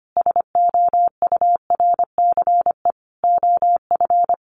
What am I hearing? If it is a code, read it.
Morse code, heard as a keyed tone, about 25 wpm, 700 Hz. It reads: SOURCE OF
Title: Sentences from Top 1200 Words 25wpm